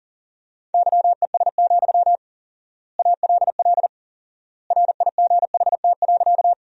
Morse code, heard as a keyed tone, about 40 wpm, 700 Hz. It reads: YES, ALL RIGHT.
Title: 4x Word-Spacing - Sentences from Top 500 Words - 40wpm